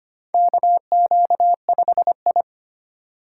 Morse code, heard as a keyed tone, about 25 wpm, 700 Hz. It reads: KQ5I
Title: USA Call Signs - 4 Characters - 3x Character Spacing - 25wpm